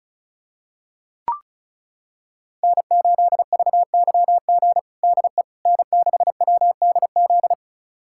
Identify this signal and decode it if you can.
Morse code, heard as a keyed tone, about 36 wpm, 700 Hz. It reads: N8VYG DE N6WDZ
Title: USA Callsign DE Callsign - 5 Characters - 35wpm